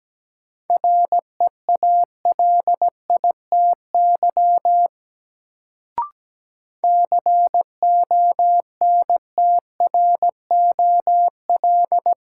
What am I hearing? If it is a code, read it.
Morse code, heard as a keyed tone, about 17 wpm, 700 Hz. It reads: REALITY CONTROL
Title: Sentences from Top 1200 Words - New Word Drill Alt - 17wpm